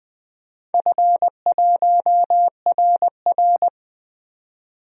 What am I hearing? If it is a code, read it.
Morse code, heard as a keyed tone, about 20 wpm, 700 Hz. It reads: F1RR
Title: DXCC-100 Call Signs - 4 Characters - 20wpm